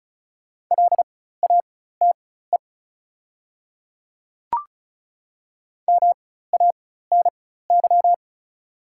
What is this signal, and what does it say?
Morse code, heard as a keyed tone, about 34 wpm, spaced as for 16 wpm, 690 Hz. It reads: LATE MANY
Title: 4x Character Spacing - Top 300 Words - 35wpm